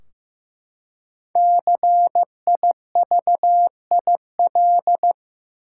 Morse code, heard as a keyed tone, about 15 wpm, 700 Hz. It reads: CIVIL
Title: Five Letter Words - 15wpm